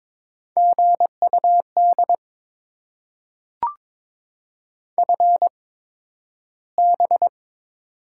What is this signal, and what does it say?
Morse code, heard as a keyed tone, about 22 wpm, spaced as for 12 wpm, 700 Hz. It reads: GUD FB
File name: Top 20 QSO Elements - 8x Character Spacing - 22wpm